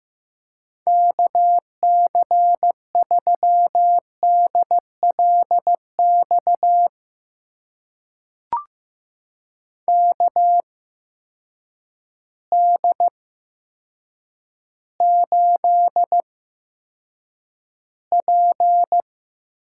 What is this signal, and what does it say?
Morse code, heard as a keyed tone, about 15 wpm, spaced as for 8 wpm, 700 Hz. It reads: KC3DLX KD8P
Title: USA Call Signs - 6 Characters - 8x Character Spacing - 15wpm